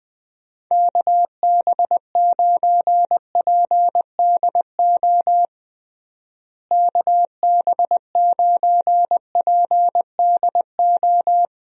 Morse code, heard as a keyed tone, about 20 wpm, 700 Hz. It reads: KB9PDO KB9PDO
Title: USA Call Signs - 6 Characters - Repeated 3x - 20wpm